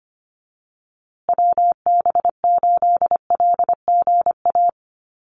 Morse code, heard as a keyed tone, about 25 wpm, 700 Hz. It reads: WB8LGA
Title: USA Call Signs Repeated 2x 25wpm